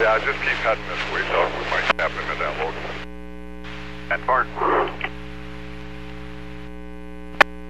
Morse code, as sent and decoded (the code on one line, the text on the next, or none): none